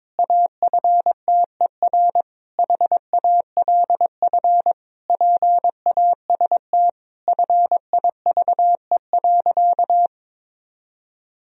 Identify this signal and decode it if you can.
Morse code, heard as a keyed tone, about 22 wpm, 700 Hz. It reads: AFTER HALF PAST FIVE.
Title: Sentences from Top 1000 Words - Review Part 3 - 22wpm